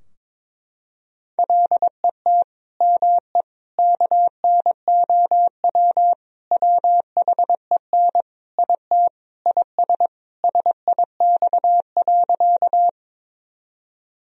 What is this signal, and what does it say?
Morse code, heard as a keyed tone, about 22 wpm, 700 Hz. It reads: LET ME KNOW WHEN IT IS SIX.
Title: Sentences from Top 1000 Words - Review Part 2 - 22wpm